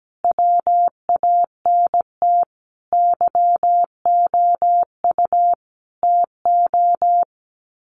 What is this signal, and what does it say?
Morse code, heard as a keyed tone, about 17 wpm, 700 Hz. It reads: WANT YOU TO
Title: Sets of 5 Words Encore2 17wpm